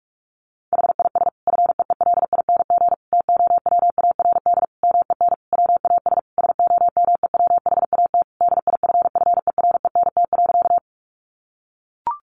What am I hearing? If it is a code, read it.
Morse code, heard as a keyed tone, about 44 wpm, 700 Hz. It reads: HIS FEELING TOWARD MEN WAS SOMEWHAT DIFFERENT.
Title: Sentences from Top 1300 Words 45wpm